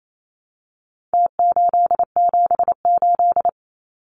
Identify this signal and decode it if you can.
Morse code, heard as a keyed tone, about 28 wpm, 700 Hz. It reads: T878
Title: Numbers - Contest Style - 4-Digit 28wpm